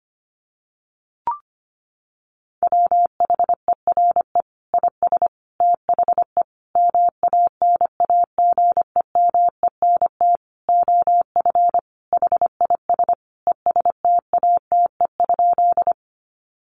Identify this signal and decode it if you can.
Morse code, heard as a keyed tone, about 25 wpm, 700 Hz. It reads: WHERE IS THE MANAGEMENT OF HIS ESTATE?
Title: Sentences from Top 1200 Words 25wpm